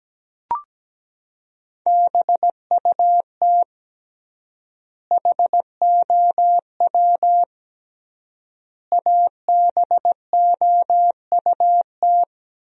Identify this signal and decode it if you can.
Morse code, heard as a keyed tone, about 17 wpm, 700 Hz. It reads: BUT HOW ABOUT
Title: 3x Word-Spacing - Sentences from Top 100 Words - 17wpm